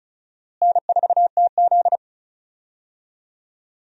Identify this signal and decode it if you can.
Morse code, heard as a keyed tone, about 36 wpm, 700 Hz. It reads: N4TZ
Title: USA Call Signs - 4 Characters - Repeated 2x - 35wpm